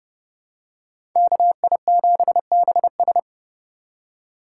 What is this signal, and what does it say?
Morse code, heard as a keyed tone, about 30 wpm, 700 Hz. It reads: KI7BS
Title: USA Call Signs - 5 Characters - 4x Character Spacing - 30wpm